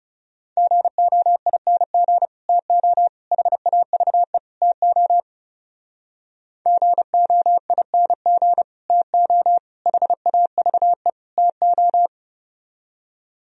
Morse code, heard as a keyed tone, about 38 wpm, 700 Hz. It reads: GOING TO HAVE TO GOING TO HAVE TO